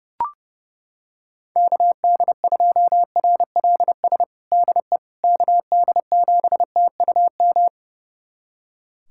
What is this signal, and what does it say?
Morse code, heard as a keyed tone, about 30 wpm, 700 Hz. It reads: KD2RLS DE KD7TUM
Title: USA Callsign DE Callsign - 6 Characters - 30wpm